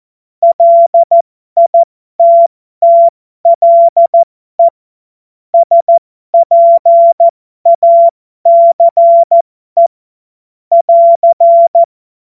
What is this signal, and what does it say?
Morse code, heard as a keyed tone, about 14 wpm, spaced as for 12 wpm, 680 Hz. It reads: LITTLE SPACE <AR>
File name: Morsekurs / Witze